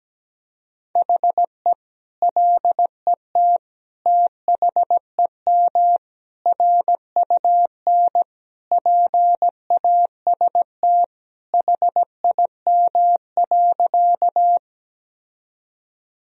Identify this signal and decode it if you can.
Morse code, heard as a keyed tone, about 17 wpm, 700 Hz. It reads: HE LET THEM RUN PAST HIM.
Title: Sentences from Top 1000 Words - Review Part 3 - 17wpm